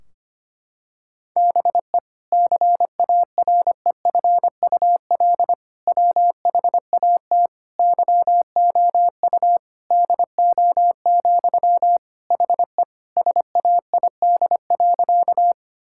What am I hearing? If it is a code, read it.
Morse code, heard as a keyed tone, about 25 wpm, 700 Hz. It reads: BE CAREFUL WHAT YOU DO, HE SAID.
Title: Sentences from Top 1600 Words 25wpm